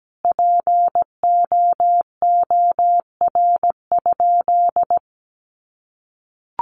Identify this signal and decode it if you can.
Morse code, heard as a keyed tone, about 17 wpm, 700 Hz. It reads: POOR?
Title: Sentences from Top 1100 Words 17wpm